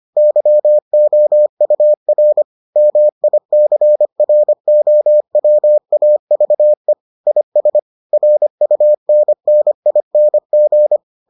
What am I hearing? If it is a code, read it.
Morse code, heard as a keyed tone, about 25 wpm, 600 Hz. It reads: YOUR MICROWAVE IS RUNNING